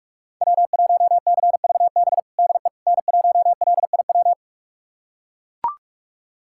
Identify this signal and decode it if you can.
Morse code, heard as a keyed tone, about 42 wpm, 700 Hz. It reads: W1CVB DE N1LIW
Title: USA Callsign DE Callsign - 5 Characters - 45wpm